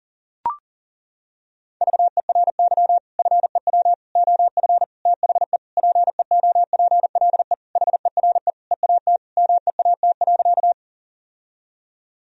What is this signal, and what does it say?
Morse code, heard as a keyed tone, about 40 wpm, 700 Hz. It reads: VERY FEW OF THE PEOPLE HERE EAT MEAT.
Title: Sentences from Top 1000 Words - Review Part 3 - 40wpm